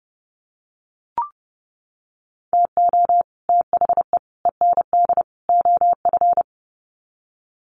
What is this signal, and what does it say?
Morse code, heard as a keyed tone, about 29 wpm, 700 Hz. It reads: TO THE END OF
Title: Sets of 4 Words 30wpm